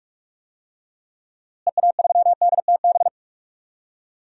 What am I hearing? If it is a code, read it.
Morse code, heard as a keyed tone, about 47 wpm, 700 Hz. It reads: EA3DTB
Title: DXCC-100 Call Signs - 6 Characters - Repeated 3x - 45wpm